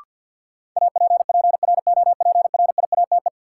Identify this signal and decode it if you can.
Morse code, heard as a keyed tone, about 53 wpm, 700 Hz. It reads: APPROPRIATE
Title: Sentences from Top 900 Words - New Word Drill - 50wpm